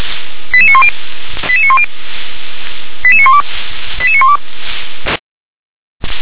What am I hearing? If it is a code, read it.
Five-tone selective call (ZVEI), heard as quick runs of five tones, tones of about 70 ms. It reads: 88910 88910 88912 88912
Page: /357/SDR/ZVEI/